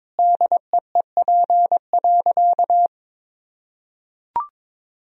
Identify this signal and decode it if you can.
Morse code, heard as a keyed tone, about 22 wpm, 700 Hz. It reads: DEEP.
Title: Sentences from Top 1000 Words - Review Part 3 - 22wpm